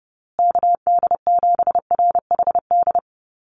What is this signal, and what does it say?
Morse code, heard as a keyed tone, about 30 wpm, 700 Hz. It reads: KD7RHD
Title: USA Call Signs Repeated 3x 30wpm